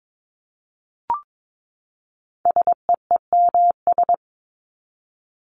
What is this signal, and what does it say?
Morse code, heard as a keyed tone, about 22 wpm, 700 Hz. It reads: SEEMS
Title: ES:S Suffix - Top 100 Words with Base Words 22wpm